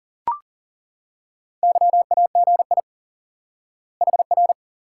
Morse code, heard as a keyed tone, about 40 wpm, 700 Hz. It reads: YAGI HR
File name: IQE Course 3 - Lesson 89 - YAGI - Two Elements - 40wpm